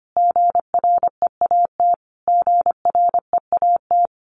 Morse code, heard as a keyed tone, about 25 wpm, 700 Hz. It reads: GREAT GREAT
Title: Top 300 Words Repeated 25wpm